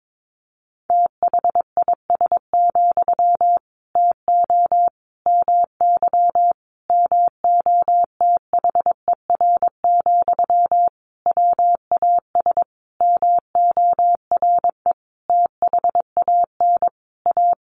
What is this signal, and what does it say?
Morse code, heard as a keyed tone, about 22 wpm, 700 Hz. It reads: THIS, TO MY MOTHER, WAS MORE THAN A
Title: Sentences from Top 1500 Words 22wpm